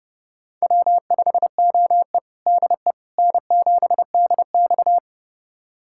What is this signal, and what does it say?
Morse code, heard as a keyed tone, about 30 wpm, 700 Hz. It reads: W5OE DE N7DX
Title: USA Callsign DE Callsign - 4 Characters - 30wpm